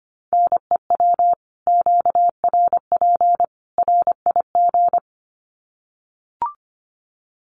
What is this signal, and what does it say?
Morse code, heard as a keyed tone, about 25 wpm, 700 Hz. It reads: NEW QRP RIG
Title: Common QSO Phrases 25wpm